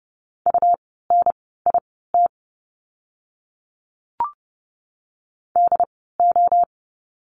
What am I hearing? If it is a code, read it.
Morse code, heard as a keyed tone, about 30 wpm, spaced as for 17 wpm, 700 Hz. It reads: UNIT DO
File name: Lesson 17f - Letter u - Words 30wpm